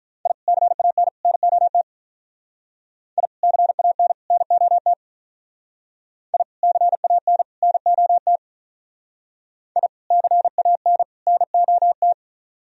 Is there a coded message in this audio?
Yes: Morse code from 0.3 to 12.1 s, heard as a keyed tone, about 45 wpm, 700 Hz. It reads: I CAN NOT I CAN NOT I CAN NOT I CAN NOT